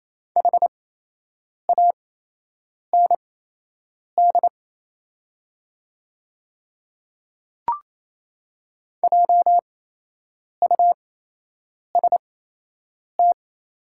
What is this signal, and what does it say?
Morse code, heard as a keyed tone, about 28 wpm, spaced as for 8 wpm, 700 Hz. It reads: HAND JUST
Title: 8x Character Spacing - Top 200 Words - 28wpm